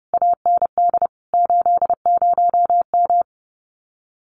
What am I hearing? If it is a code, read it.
Morse code, heard as a keyed tone, about 30 wpm, 700 Hz. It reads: AND 80M
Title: Common QSO Phrases Encore 30wpm